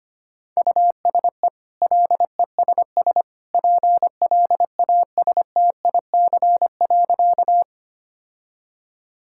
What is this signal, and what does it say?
Morse code, heard as a keyed tone, about 25 wpm, 700 Hz. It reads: USE LESS PLASTIC.
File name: Easy Sentences with Hints - 25wpm